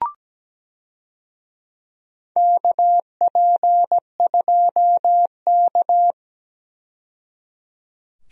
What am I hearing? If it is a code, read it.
Morse code, heard as a keyed tone, about 17 wpm, 700 Hz. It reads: KP2K
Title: DXCC-100 Call Signs - 4 Characters - 17wpm